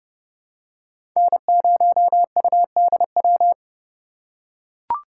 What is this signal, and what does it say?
Morse code, heard as a keyed tone, about 30 wpm, 700 Hz. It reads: N0UDW
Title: USA Call Signs - 5 Characters - 2x Character Spacing - 30wpm